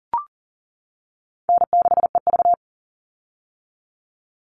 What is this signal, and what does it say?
Morse code, heard as a keyed tone, about 40 wpm, 700 Hz. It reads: N6EV
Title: Easy Call Signs - 40wpm